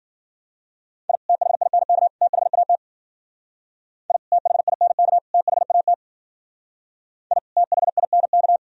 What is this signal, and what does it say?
Morse code, heard as a keyed tone, about 52 wpm, 700 Hz. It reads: I THINK THAT I THINK THAT I THINK